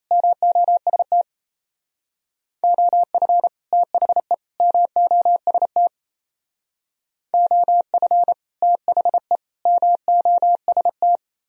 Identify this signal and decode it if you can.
Morse code, heard as a keyed tone, about 37 wpm, 700 Hz. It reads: MOST OF THE MOST OF THE MOST